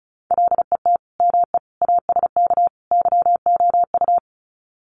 Morse code, heard as a keyed tone, about 34 wpm, 700 Hz. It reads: LET ME ASK YOU
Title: Sets of 4 Words 35wpm